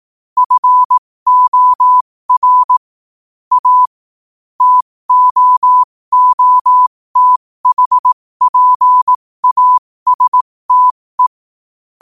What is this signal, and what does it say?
Morse code, heard as a keyed tone, about 18 wpm, spaced as for 15 wpm, 1000 Hz. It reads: FOR A TOOTHPASTE
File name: qotdfiles / 15wpm